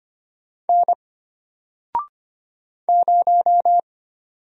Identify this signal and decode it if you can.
Morse code, heard as a keyed tone, about 25 wpm, 700 Hz. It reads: N 0
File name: Single-Letter-Number-25wpm